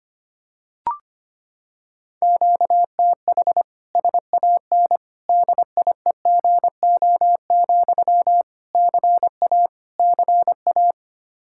Morse code, heard as a keyed tone, about 25 wpm, 700 Hz. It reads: QTH SAN DIEGO, CA CA
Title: Practice QSO - Exchange 1 - Beginning - 25wpm